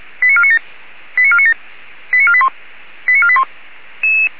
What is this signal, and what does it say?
Five-tone selective call (ZVEI), heard as quick runs of five tones, tones of about 80 ms. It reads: 89497 89497 89481 89481